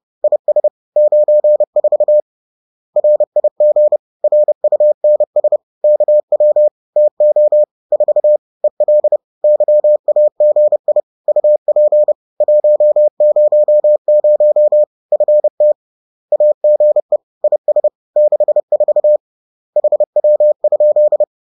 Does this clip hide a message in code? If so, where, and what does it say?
Yes: Morse code from 0.2 to 21.2 s, heard as a keyed tone, about 30 wpm, 600 Hz. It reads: IS 94 RIG RUNS KW TO 4 EL YAGI UP 100 FT AGE IS 64 HW?